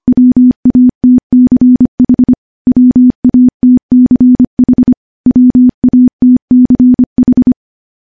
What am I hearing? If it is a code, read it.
Morse code, heard as a keyed tone, about 25 wpm, 250 Hz. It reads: WATCH WATCH WATCH
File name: morse / morse-training-25WPM